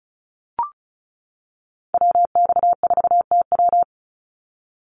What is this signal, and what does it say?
Morse code, heard as a keyed tone, about 34 wpm, 700 Hz. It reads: WX4TW
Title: NA SOTA Activator Callsigns 35wpm